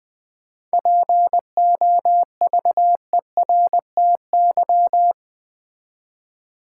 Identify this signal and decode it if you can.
Morse code, heard as a keyed tone, about 20 wpm, 700 Hz. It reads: POVERTY
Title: 7 Letter Words - First 3 Letters Given - 20wpm